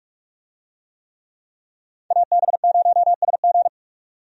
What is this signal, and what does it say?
Morse code, heard as a keyed tone, about 45 wpm, 700 Hz. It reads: AB0SG
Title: USA Call Signs - 5 Characters - 45wpm